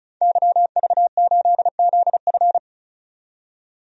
Morse code, heard as a keyed tone, about 34 wpm, 700 Hz. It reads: YV8ZF
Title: DXCC-100 Call Signs - 5 Characters - 35wpm